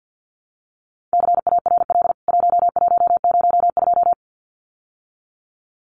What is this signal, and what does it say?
Morse code, heard as a keyed tone, about 49 wpm, 700 Hz. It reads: CARL 1102
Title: CWT Exchange 50wpm